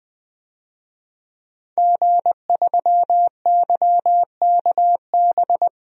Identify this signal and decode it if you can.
Morse code, heard as a keyed tone, about 21 wpm, 700 Hz. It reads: G3YKB
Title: DXCC-100 Call Signs - 5 Characters - 20wpm